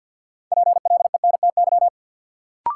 Morse code, heard as a keyed tone, about 52 wpm, 700 Hz. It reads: PLENTY
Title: Sentences from Top 1300 Words - New Word Drill Alt - 50wpm